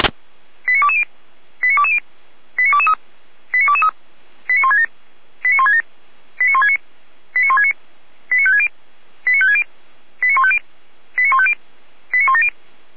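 Five-tone selective call (ZVEI), heard as quick runs of five tones, tones of about 70 ms. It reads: 89220 89220 89223 89223 89167 89167 89169 89169 89560 89560 89150 89150 89170
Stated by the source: ZVEI five-tone selective call received on an SDR, recorded by Feuerwehr Orlamünde.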